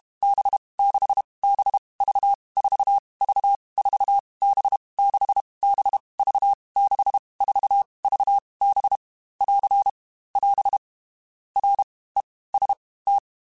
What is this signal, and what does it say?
Morse code, heard as a keyed tone, about 32 wpm, spaced as for 18 wpm, 800 Hz. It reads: B 6 B V 4 V 4 B 6 B V 6 4 V B <AR> <AS> REST
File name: Groups@30 / Groups@30 mp3s / VB46@30